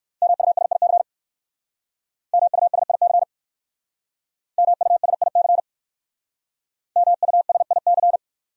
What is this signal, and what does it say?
Morse code, heard as a keyed tone, about 61 wpm, 700 Hz. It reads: MUSIC MUSIC MUSIC MUSIC